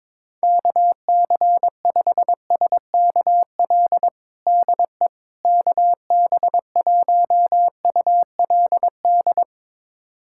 Morse code, heard as a keyed tone, about 22 wpm, 700 Hz. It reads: KC5SKL DE KB1ULD
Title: USA Callsign DE Callsign - 6 Characters - 22wpm